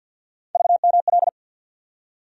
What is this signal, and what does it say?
Morse code, heard as a keyed tone, about 53 wpm, 700 Hz. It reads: VML
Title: USA Call Sign Postfixes - 3 Letters 50wpm